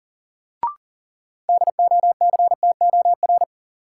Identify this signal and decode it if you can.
Morse code, heard as a keyed tone, about 40 wpm, 700 Hz. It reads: DOCTOR
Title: Sentences from Top 1500 Words - New Word Drill Alt - 40wpm